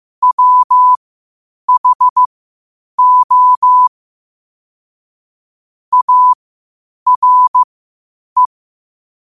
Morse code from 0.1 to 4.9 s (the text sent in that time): WHO